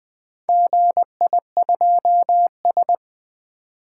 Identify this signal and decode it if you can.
Morse code, heard as a keyed tone, about 20 wpm, 700 Hz. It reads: GI2S
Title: Four Letter-Number Combinations - 20wpm